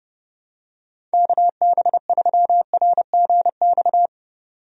Morse code, heard as a keyed tone, about 32 wpm, 700 Hz. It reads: KB3RGX
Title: USA Call Signs - 6 Characters - 30wpm